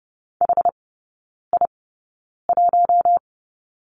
Morse code, heard as a keyed tone, about 30 wpm, spaced as for 9 wpm, 700 Hz. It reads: HI1
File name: Lesson 17e - Letter u - Three Characters 30wpm